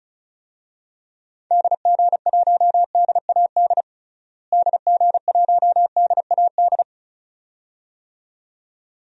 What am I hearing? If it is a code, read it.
Morse code, heard as a keyed tone, about 36 wpm, 700 Hz. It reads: DG1DAD DG1DAD
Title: DXCC-100 Call Signs - 6 Characters - Repeated 2x - 35wpm